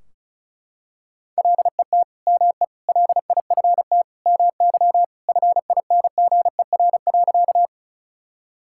Morse code, heard as a keyed tone, about 36 wpm, 700 Hz. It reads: LET ME LIFT MY FINGER.
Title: Sentences from Top 1600 Words 35wpm